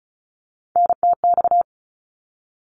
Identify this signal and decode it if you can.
Morse code, heard as a keyed tone, about 34 wpm, 700 Hz. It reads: NTX
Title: ARRL Sections 35wpm